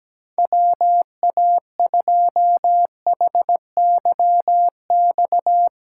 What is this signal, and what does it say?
Morse code, heard as a keyed tone, about 17 wpm, 700 Hz. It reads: WA2HYX
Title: USA Call Signs - 6 Characters - 8x Character Spacing - 17wpm